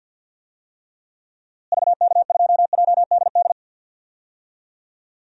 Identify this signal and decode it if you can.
Morse code, heard as a keyed tone, about 47 wpm, 700 Hz. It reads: VK2JDD
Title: DXCC-100 Call Signs - 6 Characters - Repeated 2x - 50wpm